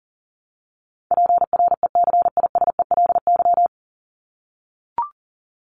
Morse code, heard as a keyed tone, about 40 wpm, 700 Hz. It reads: PRECISELY
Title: LY Suffix - Top 100 Words with Base Words 40wpm